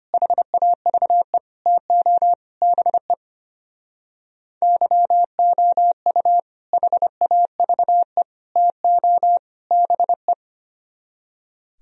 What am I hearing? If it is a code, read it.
Morse code, heard as a keyed tone, about 27 wpm, 700 Hz. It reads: HAVE TO BE YOU HAVE TO BE
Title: Speed-Racing - Sets of 4 Words 25wpm